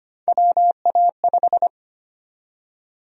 Morse code, heard as a keyed tone, about 25 wpm, 700 Hz. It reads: WA5
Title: USA Call Sign Prefixes - 25wpm